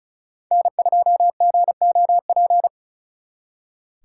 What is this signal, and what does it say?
Morse code, heard as a keyed tone, about 40 wpm, 700 Hz. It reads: N2GOP
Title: Speed Racing - USA Call Signs 35wpm